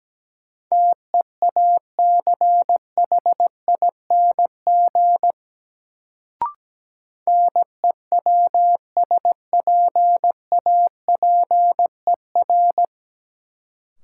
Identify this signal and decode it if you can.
Morse code, heard as a keyed tone, about 17 wpm, 700 Hz. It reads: TEACHING NEWSPAPER
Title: Sentences from Top 1100 Words - New Word Drill Alt - 17wpm